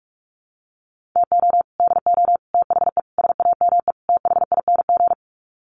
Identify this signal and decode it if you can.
Morse code, heard as a keyed tone, about 44 wpm, 700 Hz. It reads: TO DO THE SAME THING
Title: Sets of 5 Words Encore 45wpm